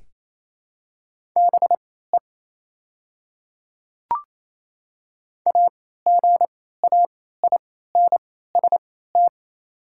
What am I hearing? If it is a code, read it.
Morse code, heard as a keyed tone, about 28 wpm, spaced as for 16 wpm, 690 Hz. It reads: BE AGAINST
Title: 3x Character Spacing - Top 500 Words - 28wpm